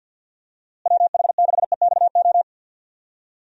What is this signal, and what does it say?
Morse code, heard as a keyed tone, about 48 wpm, 700 Hz. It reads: WH6EXO
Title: USA Call Signs - 6 Characters - Repeated 3x - 50wpm